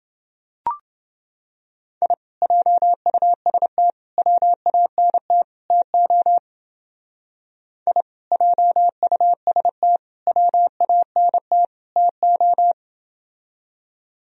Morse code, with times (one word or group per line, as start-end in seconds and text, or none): 2.0-2.1 I
2.4-3.9 JUST
4.2-5.4 WANT
5.7-6.4 TO
7.9-8.0 I
8.3-10.0 JUST
10.3-11.6 WANT
12.0-12.7 TO